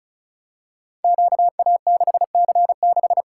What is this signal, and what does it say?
Morse code, heard as a keyed tone, about 36 wpm, 700 Hz. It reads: QA6C6
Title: Code Groups - 5 Letter:Numbers - 35wpm